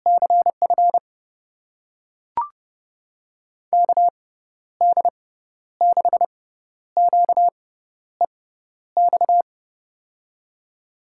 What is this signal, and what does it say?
Morse code, heard as a keyed tone, about 30 wpm, spaced as for 12 wpm, 700 Hz. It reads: CF KD6QEX
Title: USA Call Signs - 6x Character Spacing - 30wpm